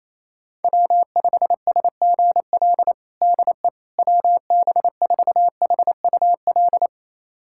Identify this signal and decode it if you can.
Morse code, heard as a keyed tone, about 28 wpm, 700 Hz. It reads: W5SGL DE WB4HUL
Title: USA Callsign DE Callsign 28wpm